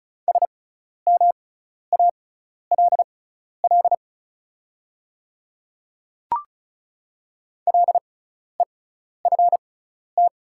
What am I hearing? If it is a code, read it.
Morse code, heard as a keyed tone, about 36 wpm, spaced as for 12 wpm, 700 Hz. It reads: SMALL LEFT
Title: 6x Character Spacing - Top 300 Words - 35wpm